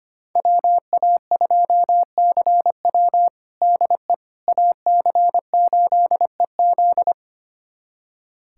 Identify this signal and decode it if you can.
Morse code, heard as a keyed tone, about 25 wpm, 700 Hz. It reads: WA2CW DE AC8EZ
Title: USA Callsign DE Callsign - 5 Characters - 25wpm